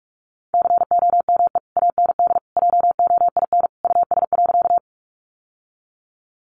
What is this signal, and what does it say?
Morse code, heard as a keyed tone, about 44 wpm, 700 Hz. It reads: COME AND JOIN US.
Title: Sentences from Top 1400 Words 45wpm